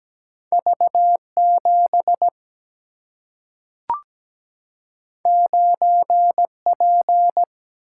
Morse code, heard as a keyed tone, about 17 wpm, 700 Hz. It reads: V 7 9 P
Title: Single-Letter Single-Number 17wpm